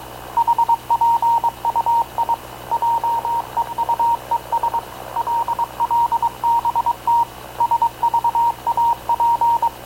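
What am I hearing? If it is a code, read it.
Morse code, heard as a keyed tone, about 22 wpm, 940 Hz. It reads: HPUI JEUES LLBT SVAP
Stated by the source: Havana Radio / CLA sending Spanish press (PX) in Morse, recorded by Radio Officer Paul Dumesnil onboard CNAV QUEST/CZDO.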